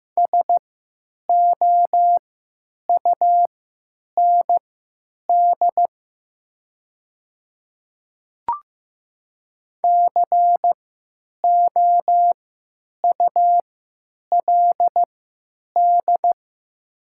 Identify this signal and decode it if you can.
Morse code, heard as a keyed tone, about 15 wpm, spaced as for 8 wpm, 700 Hz. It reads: SOUND COULD
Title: 3x Character Spacing - Top 100 Words - 15wpm